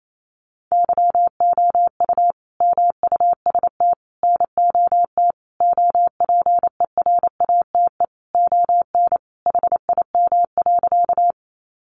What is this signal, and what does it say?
Morse code, heard as a keyed tone, about 28 wpm, 700 Hz. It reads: YOU MUST NOT OPERATE ON HIM.
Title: Sentences from Top 1500 Words 28wpm